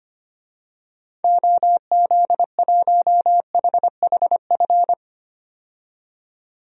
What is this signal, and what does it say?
Morse code, heard as a keyed tone, about 25 wpm, 700 Hz. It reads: OZ1HHF
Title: DXCC-100 Call Signs - 6 Characters - 25wpm